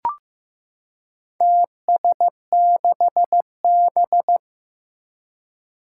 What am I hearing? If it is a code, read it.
Morse code, heard as a keyed tone, about 15 wpm, 700 Hz. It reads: TS6B
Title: Four Challenging Letter-Number Combinations 15wpm